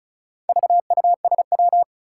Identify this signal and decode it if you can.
Morse code, heard as a keyed tone, about 36 wpm, 700 Hz. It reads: VUSW
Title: Four Challenging Letter-Number Combinations 35wpm